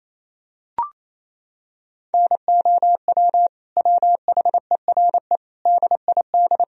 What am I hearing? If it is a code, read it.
Morse code, heard as a keyed tone, about 28 wpm, 700 Hz. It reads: NOW WHERE DID
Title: Sentences from Top 200 Words and Abbreviations 28wpm